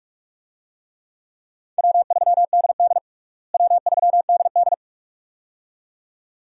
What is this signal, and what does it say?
Morse code, heard as a keyed tone, about 46 wpm, 700 Hz. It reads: W3DD W3DD
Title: USA Call Signs - 4 Characters - Repeated 2x - 45wpm